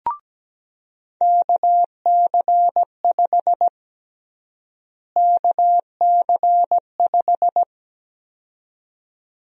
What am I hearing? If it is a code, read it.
Morse code, heard as a keyed tone, about 17 wpm, 700 Hz. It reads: KC5 KC5
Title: USA Call Sign Prefixes - Repeated 2x - 17wpm